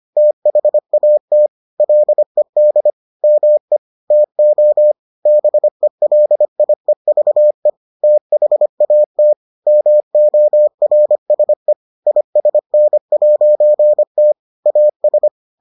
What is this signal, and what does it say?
Morse code, heard as a keyed tone, about 25 wpm, 600 Hz. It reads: THAT LED ME TO BELIEVE THAT MORSE ISN'T AS